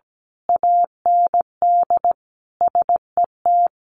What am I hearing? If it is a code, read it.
Morse code, heard as a keyed tone, about 17 wpm, 700 Hz. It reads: AND SET
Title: A Sound Of Thunder 17wpm